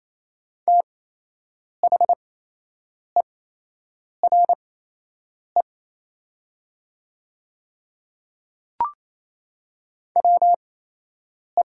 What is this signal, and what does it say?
Morse code, heard as a keyed tone, about 28 wpm, spaced as for 8 wpm, 700 Hz. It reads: THERE WE
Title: 8x Character Spacing - Top 100 Words - 28wpm